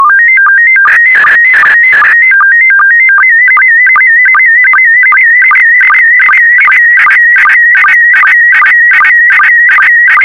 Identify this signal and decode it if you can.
SSTV (slow-scan television), PD 50.
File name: sstv / PD50-VK4MSL